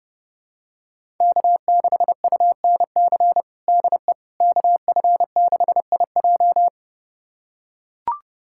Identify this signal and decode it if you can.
Morse code, heard as a keyed tone, about 30 wpm, 700 Hz. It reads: K6UNC DE KF6IJ
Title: USA Callsign DE Callsign - 5 Characters - 30wpm